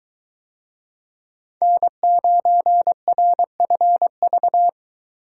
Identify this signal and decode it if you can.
Morse code, heard as a keyed tone, about 22 wpm, 700 Hz. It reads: N9RFV